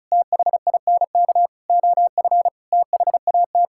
Morse code, heard as a keyed tone, about 36 wpm, 700 Hz. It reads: THINK OF THAT
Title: Sentences from Top 1000 Words - Review Part 6 - 35wpm